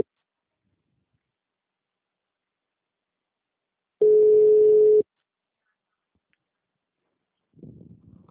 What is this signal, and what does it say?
Ringback tone (Europe, 425 Hz).
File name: out- -2006- - - .91455